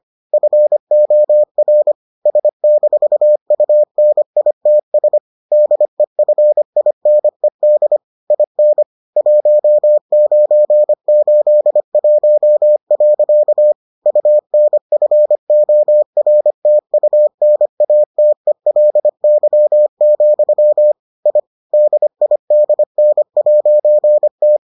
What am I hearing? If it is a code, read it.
Morse code, heard as a keyed tone, about 25 wpm, 600 Hz. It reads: FOR S-UNITS DEFINED IN 1981. UNFORTUNATELY, I DIDN'T